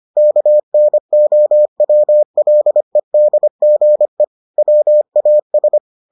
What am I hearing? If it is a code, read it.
Morse code, heard as a keyed tone, about 25 wpm, 600 Hz. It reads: KNOWLEDGE WAS